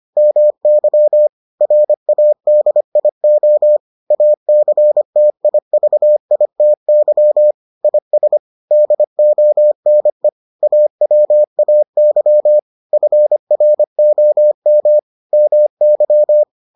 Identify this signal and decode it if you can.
Morse code, heard as a keyed tone, about 25 wpm, 600 Hz. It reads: MY RADIO ACTIVITY IS DONE AWAY FROM MY